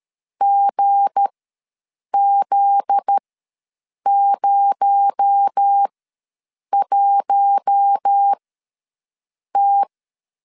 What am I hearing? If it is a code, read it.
Morse code, heard as a keyed tone, about 13 wpm, spaced as for 7 wpm, 800 Hz. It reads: GZ01T
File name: 7 GF9Z ws 12wpm